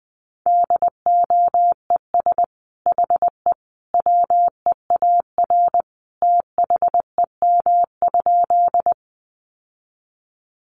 Morse code, heard as a keyed tone, about 20 wpm, 700 Hz. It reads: DOES HE WEAR THEM?
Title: Sentences from Top 1500 Words 20wpm